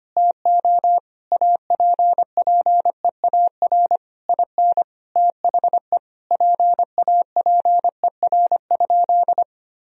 Morse code, heard as a keyed tone, about 25 wpm, 700 Hz. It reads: TO APPEAR IN THE PAPER?
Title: Sentences from Top 1000 Words - Review Part 2 - 25wpm